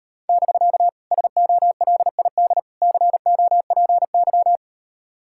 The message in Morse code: <BK> SOLID COPY

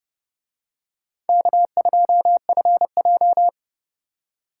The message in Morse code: K2FJ

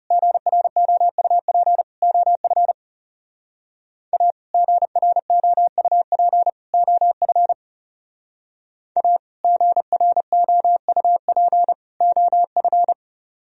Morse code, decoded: GROUP OF A GROUP OF A GROUP OF

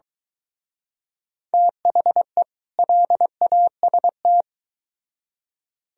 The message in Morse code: THE LAST

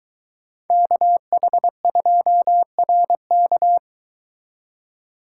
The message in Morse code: KH2RK

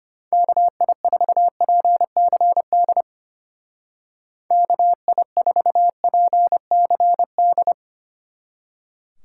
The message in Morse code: KI4PCD KI4PCD

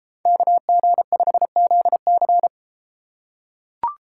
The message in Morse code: KG5ZC

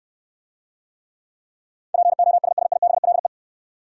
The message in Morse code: POSSIBLE

